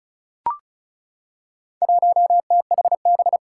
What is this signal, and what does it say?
Morse code, heard as a keyed tone, about 36 wpm, 700 Hz. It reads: 1THB
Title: Four Challenging Letter-Number Combinations 35wpm